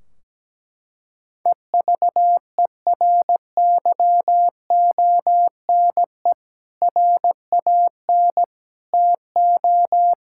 Morse code, heard as a keyed tone, about 17 wpm, 700 Hz. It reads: EVERYONE RAN TO
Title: Sentences from Top 1600 Words 17wpm